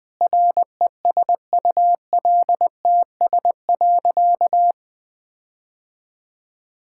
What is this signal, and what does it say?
Morse code, heard as a keyed tone, about 19 wpm, 700 Hz. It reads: RESULTS.
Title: Sentences with 7-Word Phrases - Spoken Beginning - 20wpm